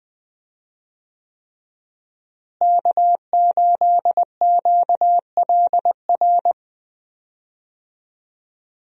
Morse code, heard as a keyed tone, about 20 wpm, 700 Hz. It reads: K8QLR
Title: USA Call Signs - 5 Characters - Repeated 3x - 20wpm